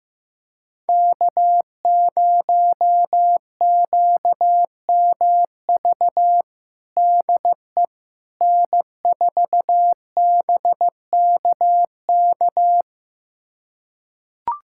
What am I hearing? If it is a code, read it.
Morse code, heard as a keyed tone, about 15 wpm, 700 Hz. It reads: K0QMV DE N4BKK
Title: USA Callsign DE Callsign - 5 Characters - 15wpm